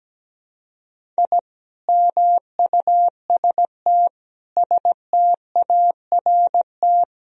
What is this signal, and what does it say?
Morse code, heard as a keyed tone, about 17 wpm, 700 Hz. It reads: I MUST START
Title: Sentences from Top 1000 Words - Review Part 1 - 17wpm